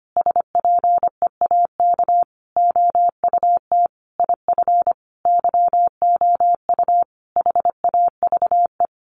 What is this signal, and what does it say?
Morse code, heard as a keyed tone, about 25 wpm, 700 Hz. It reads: SPEAK OUT IF YOU HAVE
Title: Sentences from Top 1300 Words 25wpm